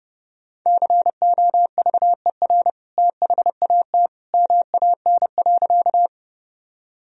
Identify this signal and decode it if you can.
Morse code, heard as a keyed tone, about 29 wpm, 700 Hz. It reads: COVER THAT MAN.
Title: Sentences from Top 1000 Words - Review Part 1 - 30wpm